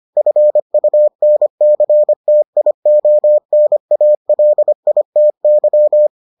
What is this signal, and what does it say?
Morse code, heard as a keyed tone, about 25 wpm, 600 Hz. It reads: FUNCTIONALITY